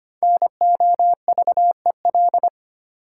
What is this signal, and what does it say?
Morse code, heard as a keyed tone, about 25 wpm, 700 Hz. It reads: NOVEL
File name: practice / 25wpm / Difficult Words with Hints - 25wpm